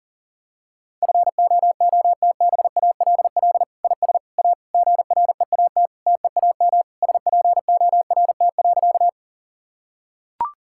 Morse code, heard as a keyed tone, about 40 wpm, 700 Hz. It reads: FOOTBALL IS A GREAT TEAM SPORT.
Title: Sentences from Top 1600 Words 40wpm